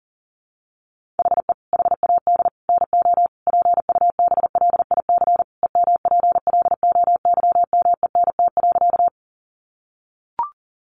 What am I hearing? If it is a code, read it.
Morse code, heard as a keyed tone, about 40 wpm, 700 Hz. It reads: HE HAD NO PUBLIC EMPLOYMENT.
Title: Sentences from Top 1500 Words 40wpm